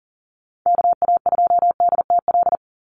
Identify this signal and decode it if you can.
Morse code, heard as a keyed tone, about 40 wpm, 700 Hz. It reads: KA2DTL
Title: USA Call Signs 40wpm